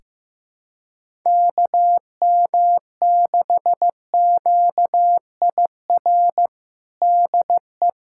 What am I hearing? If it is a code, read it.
Morse code, heard as a keyed tone, about 15 wpm, 700 Hz. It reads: KM6QIR DE
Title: USA Callsign DE Callsign 15wpm